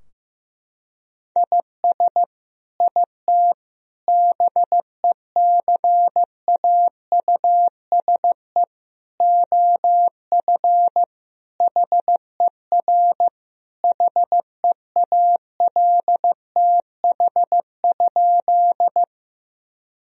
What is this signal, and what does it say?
Morse code, heard as a keyed tone, about 16 wpm, 700 Hz. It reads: IS IT BECAUSE OF HER HEALTH?
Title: Sentences from Top 1000 Words - Review Part 3 - 15wpm